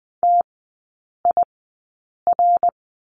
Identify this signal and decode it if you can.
Morse code, heard as a keyed tone, about 20 wpm, spaced as for 8 wpm, 700 Hz. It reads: TIR
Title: 074 Lesson 13e - Letter L - Three Characters 20wpm